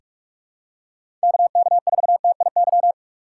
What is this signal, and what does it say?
Morse code, heard as a keyed tone, about 45 wpm, 700 Hz. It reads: KK4TIY
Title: USA Call Signs - 6 Characters - 4x Character Spacing - 45wpm